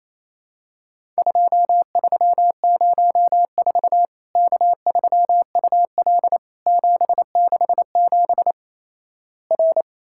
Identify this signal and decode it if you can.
Morse code, heard as a keyed tone, about 28 wpm, 700 Hz. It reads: 2304 K3UL 767 R
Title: Worked All Europe DX Contest - QTC Exchange - 28wpm